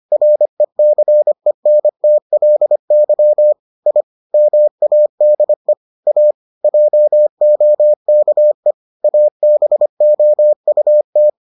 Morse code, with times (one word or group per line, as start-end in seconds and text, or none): 0.1-3.5 RECENTLY
3.9-4.0 I
4.3-5.7 MADE
6.1-6.3 A
6.6-8.7 JOKE
9.0-11.3 ABOUT